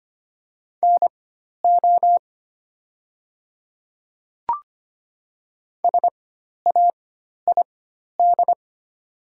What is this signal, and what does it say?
Morse code, heard as a keyed tone, about 25 wpm, spaced as for 12 wpm, 690 Hz. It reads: NO SAID
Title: 4x Character Spacing - Top 500 Words - 25wpm